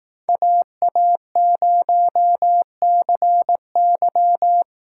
Morse code, heard as a keyed tone, about 18 wpm, 700 Hz. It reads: AA0CY
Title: Speed Racing - USA Call Signs 15wpm